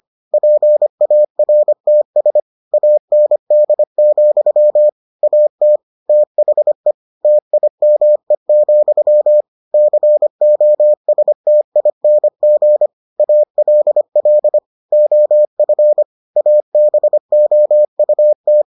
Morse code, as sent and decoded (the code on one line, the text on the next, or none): .--. .- .-. - ... / .- -. -.. --..-- / .- - / - .... / - .. -- . --..-- / -.-. --- ... - .. -. --. / .- .-.. .-.. / --- ..-. / .- -... --- ..- -
PARTS AND, AT THE TIME, COSTING ALL OF ABOUT